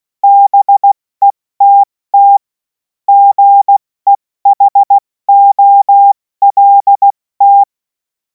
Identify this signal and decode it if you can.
Morse code, heard as a keyed tone, about 16 wpm, 800 Hz. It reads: BETT GEHOLT
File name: Morsekurs / Witze / Witz 07. Juli 2026 70 F80